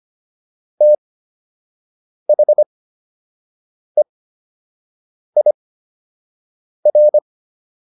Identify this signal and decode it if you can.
Morse code, heard as a keyed tone, about 25 wpm, spaced as for 6 wpm, 600 Hz. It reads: THEIR